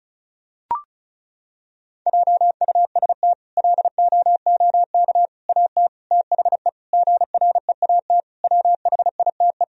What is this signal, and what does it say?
Morse code, heard as a keyed tone, about 36 wpm, 700 Hz. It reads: JUST LOOK AT THE GREAT WHITE
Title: Sentences from Top 1000 Words - Review Part 1 - 35wpm